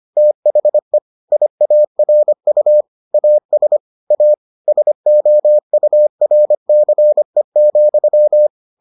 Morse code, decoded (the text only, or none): THE IARU AS A SOURCE,